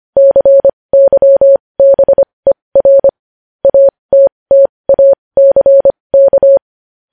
Morse code, heard as a keyed tone, about 25 wpm, spaced as for 20 wpm, 550 Hz. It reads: CYBER ATTACK
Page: News Headlines in Morse Code at 20 WPM